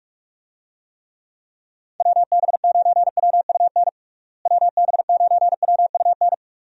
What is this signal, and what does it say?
Morse code, heard as a keyed tone, about 41 wpm, 700 Hz. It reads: WB9WUN WB9WUN
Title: USA Call Signs - 6 Characters - Repeated 2x - 45wpm